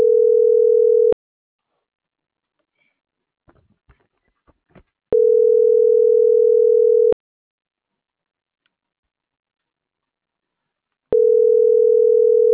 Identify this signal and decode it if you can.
Ringback tone (North America), 440 + 480 Hz.